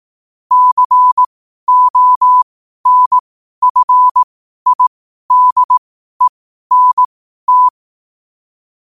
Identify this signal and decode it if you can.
Morse code, heard as a keyed tone, about 18 wpm, spaced as for 12 wpm, 1000 Hz. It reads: CONFIDENT
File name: qotdfiles / 12wpm